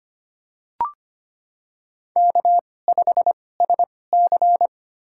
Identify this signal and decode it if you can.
Morse code, heard as a keyed tone, about 25 wpm, spaced as for 18 wpm, 700 Hz. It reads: K5SC
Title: USA Call Signs - 4 Characters - 2x Character Spacing - 25wpm